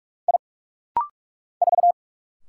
Morse code, heard as a keyed tone, about 46 wpm, 700 Hz. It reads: I 4
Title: Single Letter-Number Alt - Rapid-Fire 45wpm